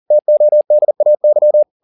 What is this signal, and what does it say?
Morse code, heard as a keyed tone, about 40 wpm, 600 Hz. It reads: TODAY